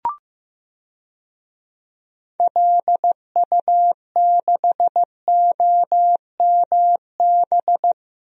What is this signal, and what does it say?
Morse code, heard as a keyed tone, about 15 wpm, 700 Hz. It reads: LU6OMB
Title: DXCC-100 Call Signs - 6 Characters - Repeated 3x - 15wpm